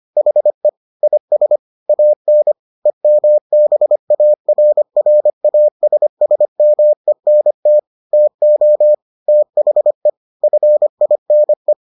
Morse code, heard as a keyed tone, about 25 wpm, 600 Hz. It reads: HE IS AN EMBARRASSMENT TO THE FINE